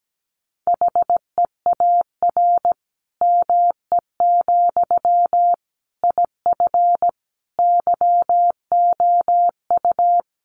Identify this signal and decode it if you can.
Morse code, heard as a keyed tone, about 17 wpm, 700 Hz. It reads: HEAR ME, IF YOU
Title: Sentences from Top 1500 Words 17wpm